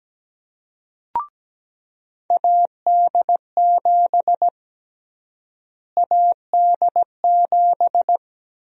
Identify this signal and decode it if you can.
Morse code, heard as a keyed tone, about 17 wpm, 700 Hz. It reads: AD7 AD7
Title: USA Call Sign Prefixes - Repeated 2x - 17wpm